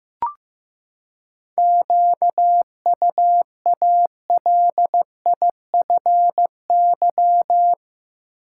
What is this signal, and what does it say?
Morse code, heard as a keyed tone, about 15 wpm, 700 Hz. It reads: QUALIFY
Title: Seven Letter Words - 15wpm